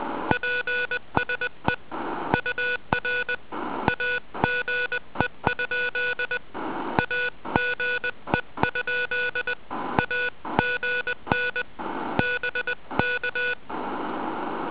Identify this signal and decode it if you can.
Morse code, heard as a keyed tone, about 20 wpm, 1470 Hz. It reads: PSE UR AGE? AGE? AGN BK